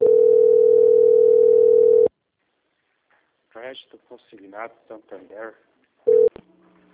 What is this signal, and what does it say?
Ringback tone (North America), 440 + 480 Hz, with speech.
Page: Guarapuava - PR